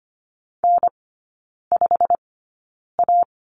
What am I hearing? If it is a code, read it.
Morse code, heard as a keyed tone, about 25 wpm, spaced as for 9 wpm, 700 Hz. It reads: N5A
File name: Lesson 15e - Number 5 - Three Characters 25wpm